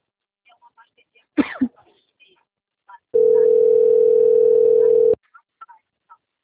Ringback tone (North America), 440 + 480 Hz.